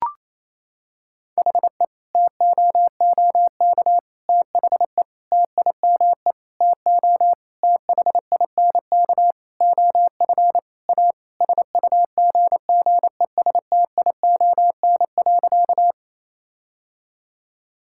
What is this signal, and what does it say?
Morse code, heard as a keyed tone, about 28 wpm, 700 Hz. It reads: HE TOOK THE TIME TO THINK OF A SUGGESTION.
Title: Sentences from Top 1600 Words 28wpm